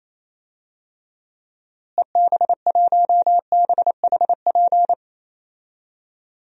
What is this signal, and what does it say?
Morse code, heard as a keyed tone, about 28 wpm, 700 Hz. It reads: EB1BHP
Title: DXCC-100 Call Signs - 6 Characters - Repeated 2x - 28wpm